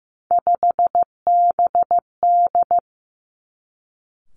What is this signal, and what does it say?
Morse code, heard as a keyed tone, about 15 wpm, 700 Hz. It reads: 5BD
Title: Three Challenging Letter-Number Combinations 15wpm